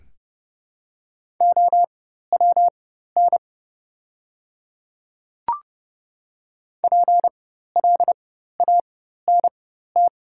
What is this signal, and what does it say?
Morse code, heard as a keyed tone, about 30 wpm, spaced as for 14 wpm, 700 Hz. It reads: OWN PLANT
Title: 4x Character Spacing - Top 500 Words - 30wpm